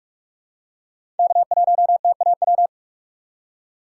Morse code, heard as a keyed tone, about 44 wpm, 700 Hz. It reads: K1TAW
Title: USA Call Signs - 5 Characters - 6x Character Spacing - 45wpm